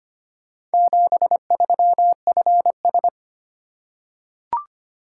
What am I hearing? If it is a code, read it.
Morse code, heard as a keyed tone, about 25 wpm, 700 Hz. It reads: 73FS
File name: practice / 25wpm / Four Letter-Number Combinations - 25wpm